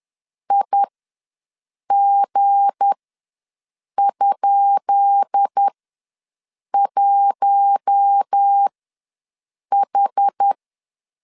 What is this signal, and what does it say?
Morse code, heard as a keyed tone, about 10 wpm, spaced as for 6 wpm, 800 Hz. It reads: IG?1H
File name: LEZIONI / 10 SIGNS / 10B INT ws 10wpm 4134